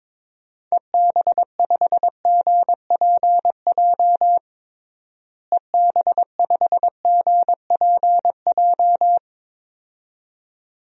Morse code, heard as a keyed tone, about 22 wpm, 700 Hz. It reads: EB5GPJ EB5GPJ
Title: DXCC-100 Call Signs - 6 Characters - Repeated 2x - 22wpm